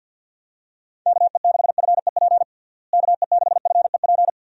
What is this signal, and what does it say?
Morse code, heard as a keyed tone, about 51 wpm, 700 Hz. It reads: KE6FEP KE6FEP
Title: USA Call Signs - 6 Characters - Repeated 2x - 50wpm